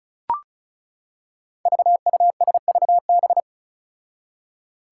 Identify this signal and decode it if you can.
Morse code, heard as a keyed tone, about 34 wpm, 700 Hz. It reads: VUSVB
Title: Five Challenging Letter-Number Combinations 35wpm